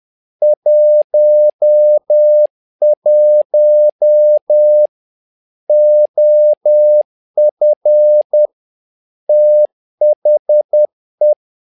Morse code, heard as a keyed tone, about 10 wpm, 600 Hz. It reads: 11 OF THE
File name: learning center / tower trouble / tower trouble 10wpm